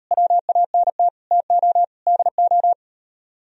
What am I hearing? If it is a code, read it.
Morse code, heard as a keyed tone, about 38 wpm, 700 Hz. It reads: WANT TO DO